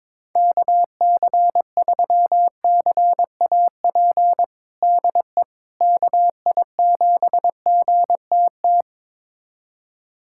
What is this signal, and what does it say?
Morse code, heard as a keyed tone, about 22 wpm, 700 Hz. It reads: KC3CAP DE KI7GTT
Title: USA Callsign DE Callsign 22wpm